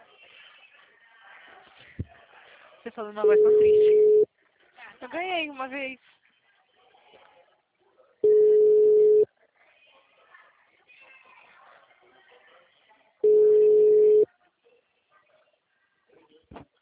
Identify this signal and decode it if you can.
Ringback tone (Europe, 425 Hz), with speech.